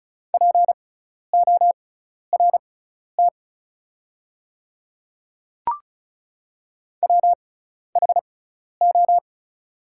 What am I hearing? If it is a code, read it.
Morse code, heard as a keyed tone, about 36 wpm, spaced as for 12 wpm, 690 Hz. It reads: PORT WHO
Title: 6x Character Spacing - Top 300 Words - 35wpm